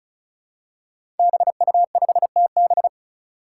Morse code, heard as a keyed tone, about 36 wpm, 700 Hz. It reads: BU5TB
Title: Five Challenging Letter-Number Combinations 35wpm